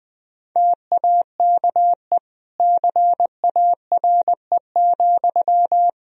Morse code, heard as a keyed tone, about 19 wpm, 700 Hz. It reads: TAKE CARE,
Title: Sentences from Top 1000 Words - Review Part 4 - 20wpm